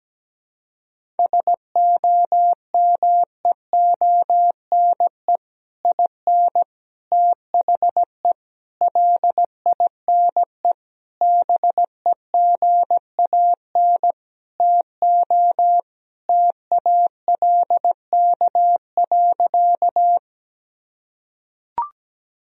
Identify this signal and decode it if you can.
Morse code, heard as a keyed tone, about 17 wpm, 700 Hz. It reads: SOMEONE IN THE LINE BEGAN TO TALK.
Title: Sentences from Top 1000 Words - Review Part 1 - 17wpm